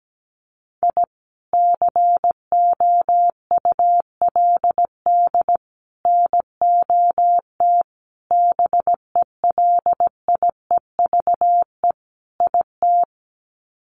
Morse code, heard as a keyed tone, about 17 wpm, 700 Hz. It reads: I COULD NOT BELIEVE IT
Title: Sets of 5 Words Encore 17wpm